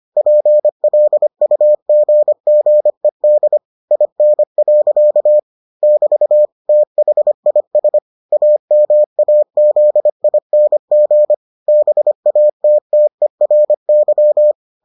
Morse code, heard as a keyed tone, about 25 wpm, 600 Hz. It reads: PLUGGED IN. <BT> THIS AMAZING BATTERY